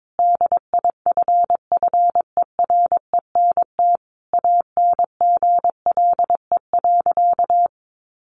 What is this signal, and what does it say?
Morse code, heard as a keyed tone, about 22 wpm, 700 Hz. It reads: DIFFERENT ANGLE.
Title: Sentences from Top 1500 Words 22wpm